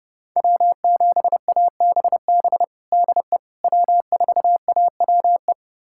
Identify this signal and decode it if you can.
Morse code, heard as a keyed tone, about 30 wpm, 700 Hz. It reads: W7ABB DE W4AWE
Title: USA Callsign DE Callsign - 5 Characters - 30wpm